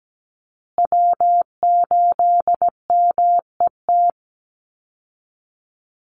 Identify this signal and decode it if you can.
Morse code, heard as a keyed tone, about 17 wpm, 700 Hz. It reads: W8MET
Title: Call Signs Repeated 2x - 17wpm